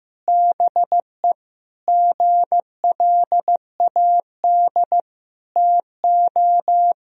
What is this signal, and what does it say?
Morse code, heard as a keyed tone, about 15 wpm, 700 Hz. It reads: BE GLAD TO
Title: Sentences from Top 1000 Words - Review Part 3 - 15wpm